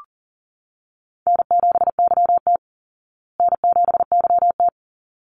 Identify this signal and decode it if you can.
Morse code, heard as a keyed tone, about 39 wpm, 700 Hz. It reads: N7YT N7YT
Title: Call Signs Repeated 3x - 40wpm